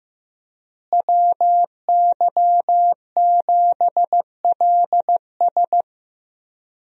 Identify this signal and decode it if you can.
Morse code, heard as a keyed tone, about 15 wpm, 700 Hz. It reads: WY7LS
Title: USA Call Signs - 5 Characters - Repeated 2x - 15wpm